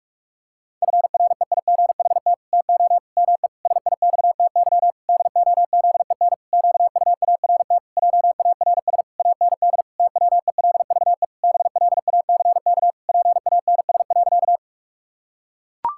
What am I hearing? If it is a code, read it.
Morse code, heard as a keyed tone, about 41 wpm, 700 Hz. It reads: FREIGHT TO ME SIXTY DOZEN QUART JARS AND TWELVE BLACK PANS.
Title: Pangram Challenge v2 - 45wpm